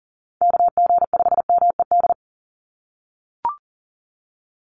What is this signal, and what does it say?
Morse code, heard as a keyed tone, about 40 wpm, 700 Hz. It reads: KG5MED
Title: USA Call Signs 40wpm